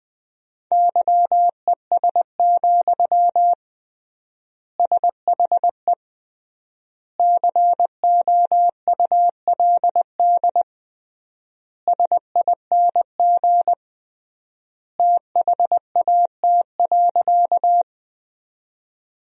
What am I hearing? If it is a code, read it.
Morse code, heard as a keyed tone, about 21 wpm, 700 Hz. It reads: YES, SHE COULD SING THAT.
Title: 3x Word-Spacing - Sentences from Top 500 Words - 20wpm